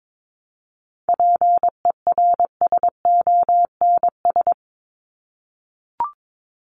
Morse code, heard as a keyed tone, about 22 wpm, 700 Hz. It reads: PERSONS
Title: ES:S Suffix - Top 100 Words with Base Words 22wpm